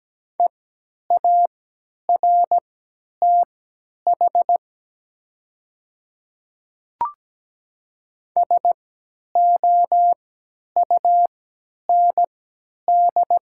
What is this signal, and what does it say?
Morse code, heard as a keyed tone, about 17 wpm, spaced as for 10 wpm, 700 Hz. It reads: EARTH SOUND
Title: 3x Character Spacing - Top 500 Words - 17wpm